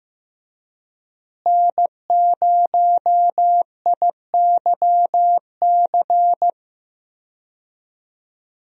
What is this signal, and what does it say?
Morse code, heard as a keyed tone, about 15 wpm, 700 Hz. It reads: N0IYC
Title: USA Call Signs - 5 Characters - Repeated 3x - 15wpm